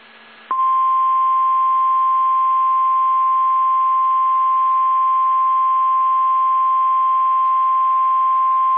Emergency Alert System (SAME) tones (the 1050 Hz attention tone). It is heard from 1.1 to 8.3 s.
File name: eas / KSDZKDJL / KSDZKDJL